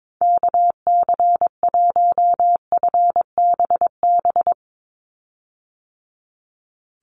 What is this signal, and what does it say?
Morse code, heard as a keyed tone, about 22 wpm, 700 Hz. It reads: KC1FBB
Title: USA Call Signs 22wpm